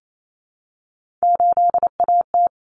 Morse code, heard as a keyed tone, about 28 wpm, 700 Hz. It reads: 8AT
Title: Common Cut Numbers - 3-Digit 28wpm